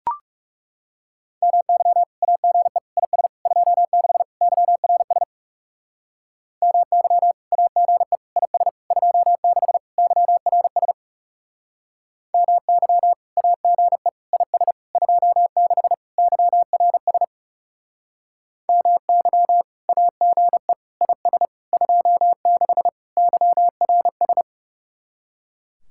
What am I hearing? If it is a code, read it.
Morse code, heard as a keyed tone, about 37 wpm, 700 Hz. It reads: MY AGE IS 26 YRS MY AGE IS 26 YRS MY AGE IS 26 YRS MY AGE IS 26 YRS
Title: Speed-Racing Common QSO Phrases - 30wpm